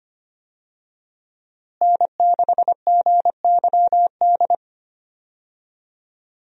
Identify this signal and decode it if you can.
Morse code, heard as a keyed tone, about 25 wpm, 700 Hz. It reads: N6GYD
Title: USA Call Signs - 5 Characters - Repeated 2x - 25wpm